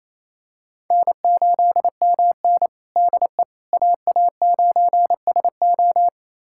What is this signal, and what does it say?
Morse code, heard as a keyed tone, about 27 wpm, 700 Hz. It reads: N8MN DE AA9SO
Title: USA Callsign DE Callsign 28wpm